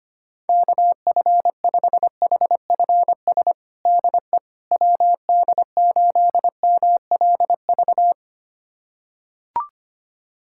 Morse code, heard as a keyed tone, about 25 wpm, 700 Hz. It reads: KF5HFS DE WD8MLV
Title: USA Callsign DE Callsign - 6 Characters - 25wpm